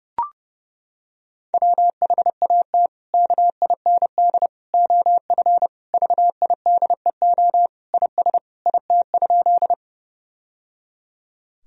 Morse code, heard as a keyed tone, about 30 wpm, 700 Hz. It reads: WHAT KIND OF VIDEO IS IT?
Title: Sentences from Top 1000 Words - Review Part 4 - 30wpm